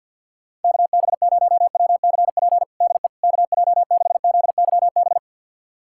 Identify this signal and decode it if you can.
Morse code, heard as a keyed tone, about 52 wpm, 700 Hz. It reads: KB0WCP DE KJ6ZYB
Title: USA Callsign DE Callsign - 6 Characters - 50wpm